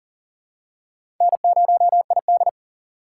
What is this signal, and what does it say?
Morse code, heard as a keyed tone, about 37 wpm, 700 Hz. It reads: N0ID
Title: USA Call Signs - 4x Character Spacing - 40wpm